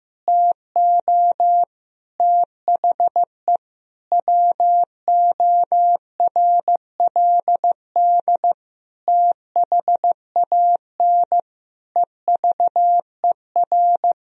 Morse code, heard as a keyed tone, about 15 wpm, 700 Hz. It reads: TO THE WORLD THAN EVER
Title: Sentences from Top 1000 Words - Review Part 3 - 15wpm